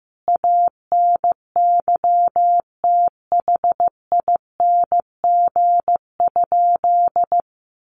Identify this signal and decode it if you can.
Morse code, heard as a keyed tone, about 15 wpm, 700 Hz. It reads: ANYTHING?
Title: Sentences from Top 1100 Words 15wpm